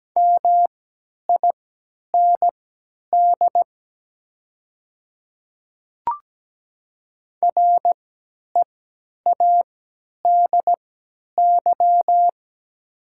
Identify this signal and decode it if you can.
Morse code, heard as a keyed tone, about 17 wpm, spaced as for 10 wpm, 700 Hz. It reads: MIND READY
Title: 3x Character Spacing - Top 500 Words - 17wpm